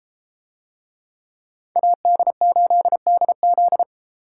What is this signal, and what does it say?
Morse code, heard as a keyed tone, about 31 wpm, 700 Hz. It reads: AD8DZ